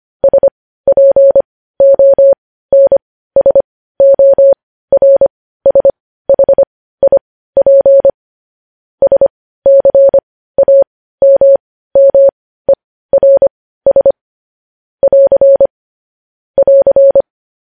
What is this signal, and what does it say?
Morse code, heard as a keyed tone, about 25 wpm, spaced as for 15 wpm, 550 Hz. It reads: SPONSORSHIP SCAMMERS <AR> <AR>
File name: podcast / News headlines-1x-25x15